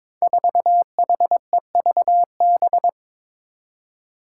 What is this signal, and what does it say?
Morse code, heard as a keyed tone, about 22 wpm, 700 Hz. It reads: 4HEVB
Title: Five Challenging Letter-Number Combinations 22wpm